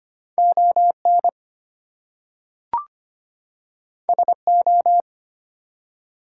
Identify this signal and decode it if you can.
Morse code, heard as a keyed tone, about 25 wpm, 700 Hz. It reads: ON SO
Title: Two Letter Words - 25wpm